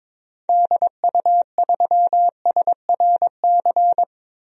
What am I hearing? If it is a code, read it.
Morse code, heard as a keyed tone, about 22 wpm, 700 Hz. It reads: DU3SRC
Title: DXCC-100 Call Signs - 6 Characters - 22wpm